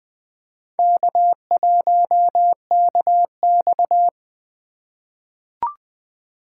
Morse code, heard as a keyed tone, about 20 wpm, 700 Hz. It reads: K1KX
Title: USA Call Signs - 4 Characters - 3x Character Spacing - 20wpm